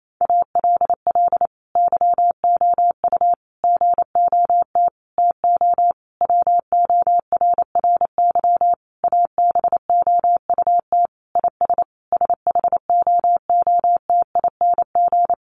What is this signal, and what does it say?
Morse code, heard as a keyed tone, about 28 wpm, 700 Hz. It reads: ALL YOU GOT TO WORRY ABOUT IS SHOOTING
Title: A Sound Of Thunder 28wpm